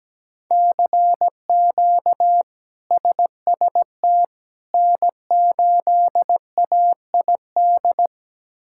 Morse code, heard as a keyed tone, about 17 wpm, 700 Hz. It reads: CQ SST N8AID
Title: K1USN SST QSO Practice - 17wpm